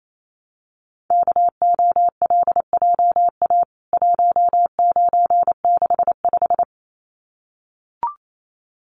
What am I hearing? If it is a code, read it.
Morse code, heard as a keyed tone, about 28 wpm, 700 Hz. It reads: KOLJA 1965
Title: CWT Exchange 28wpm